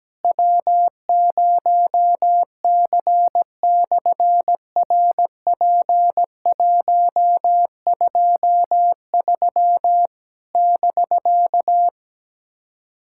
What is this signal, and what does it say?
Morse code, heard as a keyed tone, about 17 wpm, 700 Hz. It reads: W0C/RP123 <BK>
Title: SOTA QSO Practice 17wpm